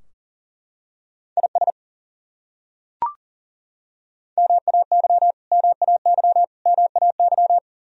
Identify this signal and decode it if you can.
Morse code, heard as a keyed tone, about 38 wpm, 700 Hz. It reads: IS MAY MAY MAY
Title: Top 100 Words Repeated 40wpm